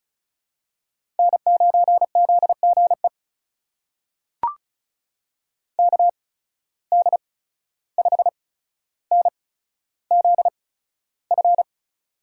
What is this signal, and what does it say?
Morse code, heard as a keyed tone, about 36 wpm, spaced as for 14 wpm, 700 Hz. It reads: N9ZGE KD5NZF
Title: USA Call Signs - 8x Character Spacing - 35wpm